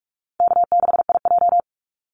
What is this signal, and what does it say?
Morse code, heard as a keyed tone, about 44 wpm, 700 Hz. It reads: K6IJ
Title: Easy Call Signs Repeated 2x - 45wpm